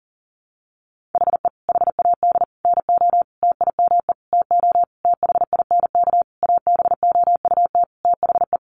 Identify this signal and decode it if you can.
Morse code, heard as a keyed tone, about 40 wpm, 700 Hz. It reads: HE HAD NO TIME TO THINK ABOUT THE
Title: Sentences from Top 700 Words 40wpm